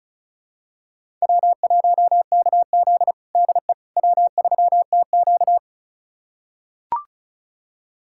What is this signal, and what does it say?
Morse code, heard as a keyed tone, about 34 wpm, 700 Hz. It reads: W1KZ DE W3TQ
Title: USA Callsign DE Callsign - 4 Characters - 35wpm